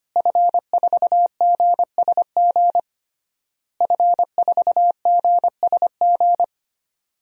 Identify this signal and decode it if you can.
Morse code, heard as a keyed tone, about 25 wpm, 700 Hz. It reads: F4GSG F4GSG
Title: DXCC-100 Call Signs - 5 Characters - Repeated 3x - 25wpm